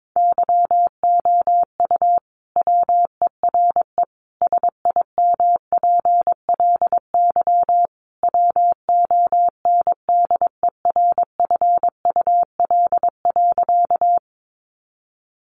Morse code, heard as a keyed tone, about 22 wpm, 700 Hz. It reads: YOU WERE SIMPLY WONDERFUL.
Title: Sentences from Top 1100 Words 22wpm